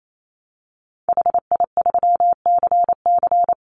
Morse code, heard as a keyed tone, about 28 wpm, 700 Hz. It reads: HI3CC
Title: Call Signs Repeated 3x - 28wpm